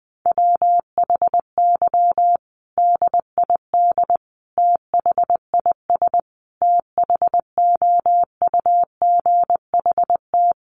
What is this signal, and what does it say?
Morse code, heard as a keyed tone, about 20 wpm, 700 Hz. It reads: WHY DID THIS THOUGHT
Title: Sentences from Top 900 Words 20wpm